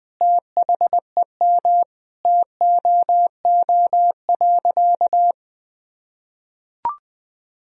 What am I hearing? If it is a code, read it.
Morse code, heard as a keyed tone, about 20 wpm, 700 Hz. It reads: THEM TOO.
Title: Sentences from Top 1000 Words - Review Part 3 - 20wpm